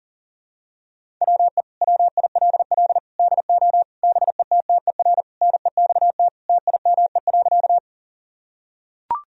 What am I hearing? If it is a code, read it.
Morse code, heard as a keyed tone, about 37 wpm, 700 Hz. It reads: WE WILL DO BETTER NEXT TIME.
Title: Sentences from Top 1000 Words - Review Part 1 - 40wpm